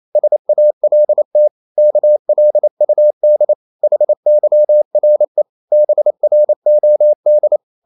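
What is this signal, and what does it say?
Morse code, heard as a keyed tone, about 28 wpm, 600 Hz. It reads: SALT KLUD HYRE BROD
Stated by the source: Danish short words sent in Morse at 28 wpm, from CW med Gnister.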